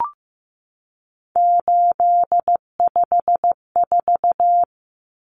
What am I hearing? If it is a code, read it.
Morse code, heard as a keyed tone, about 15 wpm, 700 Hz. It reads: 854
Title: Common Cut Numbers - 3-Digit 15wpm